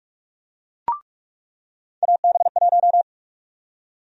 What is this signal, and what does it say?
Morse code, heard as a keyed tone, about 41 wpm, 700 Hz. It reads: AB1
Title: USA Call Sign Prefixes - 45wpm